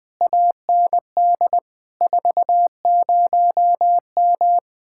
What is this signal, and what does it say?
Morse code, heard as a keyed tone, about 20 wpm, 700 Hz. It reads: AND 40M
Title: Speed-Racing Common QSO Phrases - 17wpm